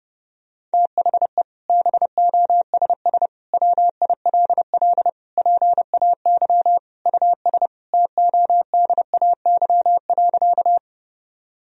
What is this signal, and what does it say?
Morse code, heard as a keyed tone, about 32 wpm, 700 Hz. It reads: THE BOSS WILL PAY US TODAY.
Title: Sentences from Top 1000 Words - Review Part 4 - 30wpm